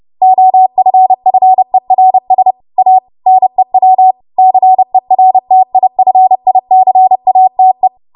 Morse code, heard as a keyed tone, about 30 wpm, 750 Hz. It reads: OFFERS A NEW CERTIFICATE
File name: Morse / 30WPM